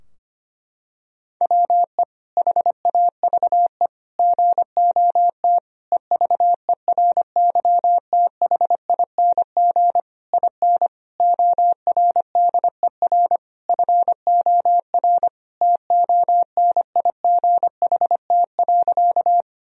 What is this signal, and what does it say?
Morse code, heard as a keyed tone, about 25 wpm, 700 Hz. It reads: WE HAVE GOT EVERYTHING IN ORDER FOR TONIGHT.
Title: Sentences from Top 1000 Words - Review Part 3 - 25wpm